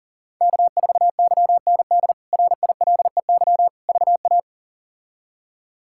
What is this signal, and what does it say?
Morse code, heard as a keyed tone, about 37 wpm, 700 Hz. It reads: K4YND RILEY VA
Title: K1USN SST QSO Practice - 40wpm